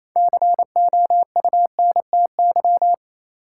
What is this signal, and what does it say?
Morse code, heard as a keyed tone, about 27 wpm, 700 Hz. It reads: COUNTY
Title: Six Letter Words - 28wpm